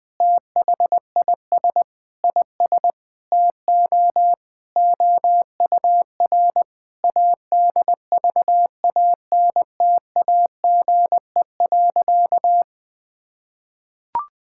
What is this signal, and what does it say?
Morse code, heard as a keyed tone, about 20 wpm, 700 Hz. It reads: THIS IS TO OUR ADVANTAGE.
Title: Sentences from Top 1000 Words - Review Part 6 - 20wpm